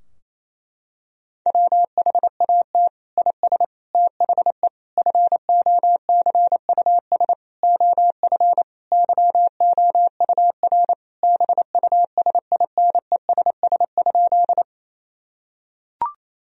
Morse code, heard as a keyed tone, about 28 wpm, 700 Hz. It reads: WHAT IS THE FOCUS OF YOUR BUSINESS?
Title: Sentences from Top 1000 Words - Review Part 4 - 28wpm